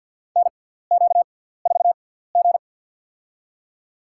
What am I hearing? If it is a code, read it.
Morse code, heard as a keyed tone, about 49 wpm, spaced as for 17 wpm, 700 Hz. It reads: NQ4G
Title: USA Call Signs - 4 Characters - 6x Character Spacing - 50wpm